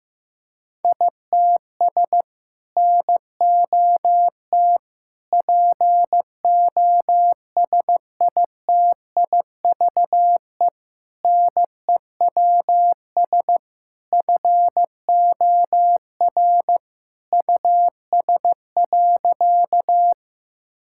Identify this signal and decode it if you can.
Morse code, heard as a keyed tone, about 15 wpm, 700 Hz. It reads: ITS NOT POSITIVE NEWS FOR US.
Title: Sentences from Top 1000 Words - Review Part 5 - 15wpm